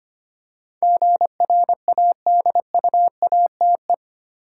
Morse code, heard as a keyed tone, about 25 wpm, 700 Hz. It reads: GRADUATE
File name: Eight Letter Words - 25wpm